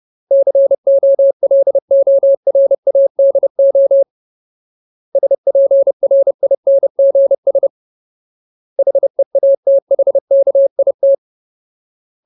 Morse code, heard as a keyed tone, about 30 wpm, 550 Hz. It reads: COLORADO SPRINGS HEATHKIT
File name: Ultimate Word List / 30WPM Ultimate